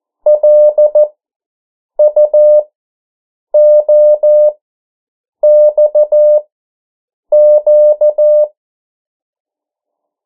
Morse code, heard as a keyed tone, about 14 wpm, spaced as for 7 wpm, 600 Hz. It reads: LUOXQ
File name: morse mp3 / 09 urok 8